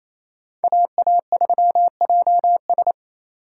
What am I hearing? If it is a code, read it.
Morse code, heard as a keyed tone, about 28 wpm, 700 Hz. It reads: AA3JS
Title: USA Call Signs - 3x Character Spacing - 28wpm